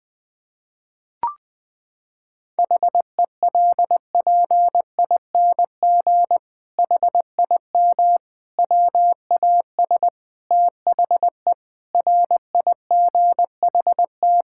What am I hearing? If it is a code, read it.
Morse code, heard as a keyed tone, about 19 wpm, 700 Hz. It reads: HELPING HIM WAS THE RIGHT
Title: Sentences with 7-Word Phrases - 20wpm